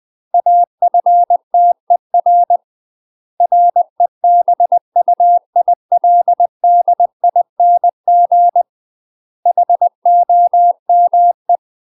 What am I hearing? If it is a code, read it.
Morse code, heard as a keyed tone, about 20 wpm, 700 Hz. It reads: AFTER REBUILDING HOME
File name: UK MASTER QSOs TRAINING / 20WPM / UK MASTER QSOs 0024